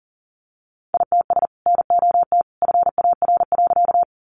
Morse code, heard as a keyed tone, about 40 wpm, 700 Hz. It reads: ITS NOT FAR.
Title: Sentences from Top 600 Words 40wpm